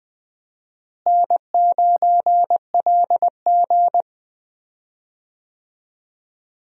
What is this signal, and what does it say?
Morse code, heard as a keyed tone, about 20 wpm, 700 Hz. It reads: N9LG
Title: USA Call Signs - 4 Characters - Repeated 2x - 20wpm